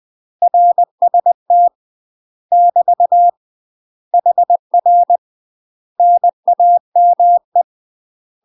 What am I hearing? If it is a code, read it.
Morse code, heard as a keyed tone, about 20 wpm, 700 Hz. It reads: RST <BT> HR NAME
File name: UK MASTER QSOs TRAINING / 20WPM / UK MASTER QSOs 0051